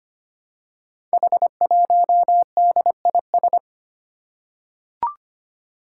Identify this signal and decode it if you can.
Morse code, heard as a keyed tone, about 25 wpm, 700 Hz. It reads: H1DIS
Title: Five Challenging Letter-Number Combinations 25wpm